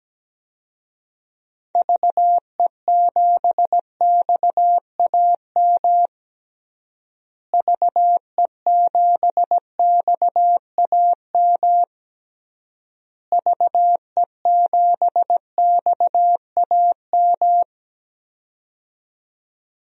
Morse code, heard as a keyed tone, about 17 wpm, 700 Hz. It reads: VE7XAM VE7XAM VE7XAM
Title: DXCC-100 Call Signs - 6 Characters - Repeated 3x - 17wpm